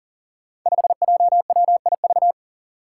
Morse code, heard as a keyed tone, about 42 wpm, 700 Hz. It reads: 5JWIV
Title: Five Challenging Letter-Number Combinations 40wpm